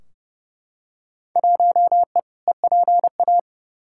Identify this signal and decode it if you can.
Morse code, heard as a keyed tone, about 30 wpm, 700 Hz. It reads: 1E EPA
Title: Field Day Exchanges - 30wpm